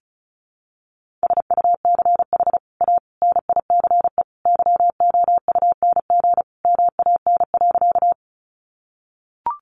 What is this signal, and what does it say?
Morse code, heard as a keyed tone, about 34 wpm, 700 Hz. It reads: SUCH A NICE YOUNG MAN.
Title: Sentences from Top 1100 Words 35wpm